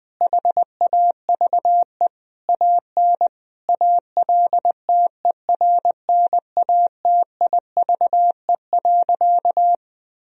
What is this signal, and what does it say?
Morse code, heard as a keyed tone, about 20 wpm, 700 Hz. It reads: HAVE AN ALTERNATIVE.
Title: Sentences from Top 1000 Words - Review Part 4 - 20wpm